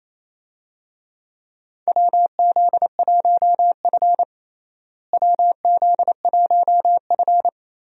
Morse code, heard as a keyed tone, about 28 wpm, 700 Hz. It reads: WZ1F WZ1F
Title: USA Call Signs - 4 Characters - Repeated 3x - 28wpm